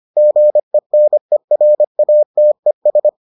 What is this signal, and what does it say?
Morse code, heard as a keyed tone, about 25 wpm, 600 Hz. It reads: GENERATES